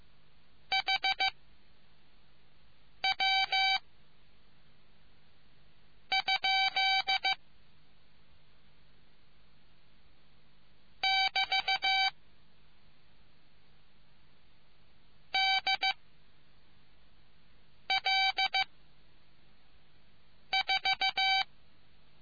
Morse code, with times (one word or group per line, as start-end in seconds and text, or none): 0.7-1.3 H
3.0-3.8 W
6.1-7.3 ?
11.0-12.1 <BT>
15.3-15.9 D
17.9-18.6 L
20.5-21.4 4